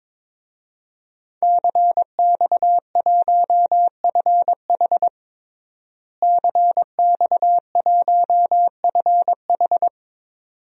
Morse code, heard as a keyed tone, about 22 wpm, 700 Hz. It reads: CX1FH CX1FH
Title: DXCC-100 Call Signs - 5 Characters - Repeated 2x - 22wpm